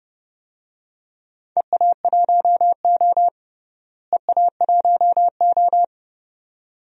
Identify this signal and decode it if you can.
Morse code, heard as a keyed tone, about 30 wpm, 700 Hz. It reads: EA1O EA1O
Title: DXCC-100 Call Signs - 4 Characters - Repeated 2x - 30wpm